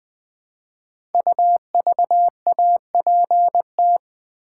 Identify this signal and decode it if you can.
Morse code, heard as a keyed tone, about 20 wpm, 700 Hz. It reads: UVAPT
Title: Code Groups - 5 Letter:Numbers - 20wpm